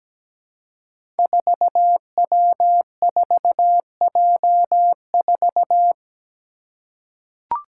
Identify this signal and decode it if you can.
Morse code, heard as a keyed tone, about 17 wpm, 700 Hz. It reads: 4W4J4
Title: Five Challenging Letter-Number Combinations 17wpm